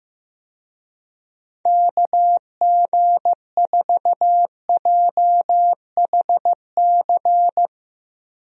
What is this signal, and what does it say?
Morse code, heard as a keyed tone, about 15 wpm, 700 Hz. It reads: KG4JHC
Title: USA Call Signs - 6 Characters - Repeated 2x - 15wpm